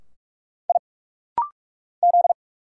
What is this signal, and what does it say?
Morse code, heard as a keyed tone, about 46 wpm, 700 Hz. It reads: I Z
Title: Single Letter-Number Alt - Rapid-Fire 45wpm